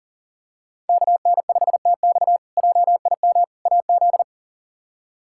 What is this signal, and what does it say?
Morse code, heard as a keyed tone, about 40 wpm, 700 Hz. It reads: KN5TX JIM AZ
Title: K1USN SST QSO Practice - 40wpm